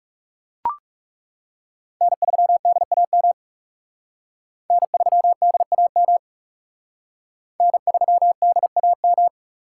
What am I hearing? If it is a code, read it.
Morse code, heard as a keyed tone, about 39 wpm, 700 Hz. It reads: N3DAM N3DAM N3DAM